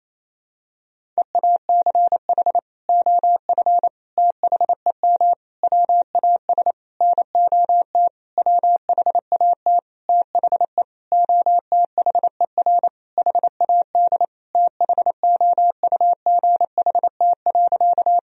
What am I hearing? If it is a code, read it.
Morse code, heard as a keyed tone, about 28 wpm, 700 Hz. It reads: EACH OF THEM WAS NOT WHAT THE OTHER HAD THOUGHT.
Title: Sentences from Top 1000 Words - Review Part 1 - 28wpm